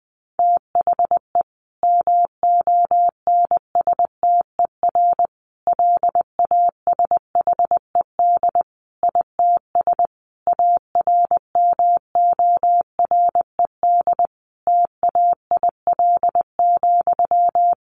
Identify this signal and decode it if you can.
Morse code, heard as a keyed tone, about 20 wpm, 700 Hz. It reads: THE MONSTER LASHED ITS ARMORED TAIL,
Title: A Sound Of Thunder 20wpm